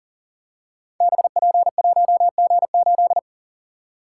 Morse code, heard as a keyed tone, about 42 wpm, 690 Hz. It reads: BP1G8
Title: Code Groups - 5 Letter:Numbers - 40wpm